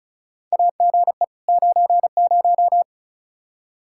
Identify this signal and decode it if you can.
Morse code, heard as a keyed tone, about 36 wpm, 700 Hz. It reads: AGE 90
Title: K1USN SST QSO Practice 35wpm